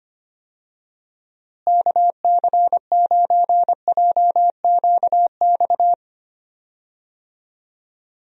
Morse code, heard as a keyed tone, about 25 wpm, 700 Hz. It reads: KC9JQX
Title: USA Call Signs - 6 Characters - Repeated 3x - 25wpm